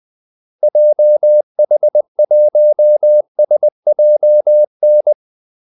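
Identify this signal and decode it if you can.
Morse code, heard as a keyed tone, about 20 wpm, 600 Hz. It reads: JH1SJN
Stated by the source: Morse call-sign practice at 20 wpm.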